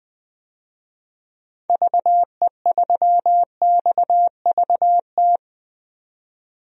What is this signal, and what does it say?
Morse code, heard as a keyed tone, about 20 wpm, 700 Hz. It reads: VE3XVT
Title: DXCC-100 Call Signs - 6 Characters - 20wpm